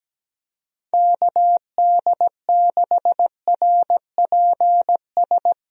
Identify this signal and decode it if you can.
Morse code, heard as a keyed tone, about 17 wpm, 700 Hz. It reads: KD6RPS
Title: USA Call Signs - 6 Characters - 17wpm